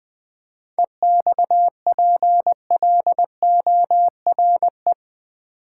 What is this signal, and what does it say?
Morse code, heard as a keyed tone, about 21 wpm, 700 Hz. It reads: EXPLORE
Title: Sentences from Top 1100 Words - New Word Drill v2 - 20wpm